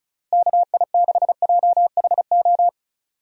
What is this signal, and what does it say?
Morse code, heard as a keyed tone, about 34 wpm, 700 Hz. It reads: KI6JHO
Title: USA Call Signs - 6 Characters - Repeated 2x - 35wpm